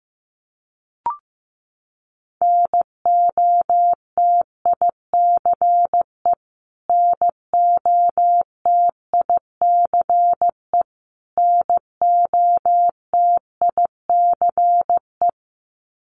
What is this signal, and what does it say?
Morse code, heard as a keyed tone, about 15 wpm, 700 Hz. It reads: NOTICE NOTICE NOTICE
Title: Top 500 Words Repeated 15wpm